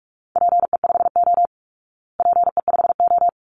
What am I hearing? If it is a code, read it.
Morse code, heard as a keyed tone, about 44 wpm, 700 Hz. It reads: PE5O PE5O
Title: Call Signs Repeated 3x - 45wpm